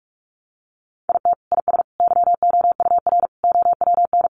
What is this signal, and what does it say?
Morse code, heard as a keyed tone, about 44 wpm, 700 Hz. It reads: IT IS YOUR OWN
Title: Sentences from Top 600 Words 45wpm